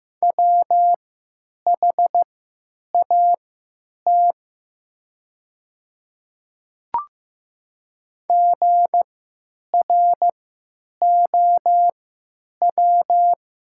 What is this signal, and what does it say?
Morse code, heard as a keyed tone, about 15 wpm, spaced as for 9 wpm, 700 Hz. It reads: WHAT GROW
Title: 3x Character Spacing - Top 300 Words - 15wpm